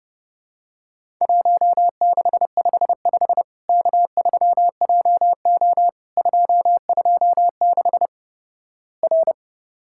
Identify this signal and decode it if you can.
Morse code, heard as a keyed tone, about 32 wpm, 700 Hz. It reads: 1655 K3JO 226 R
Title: Worked All Europe DX Contest - QTC Exchange - 30wpm